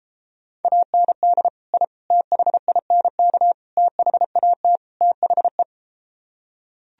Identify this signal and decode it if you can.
Morse code, heard as a keyed tone, about 33 wpm, 700 Hz. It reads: AND I THINK THAT THE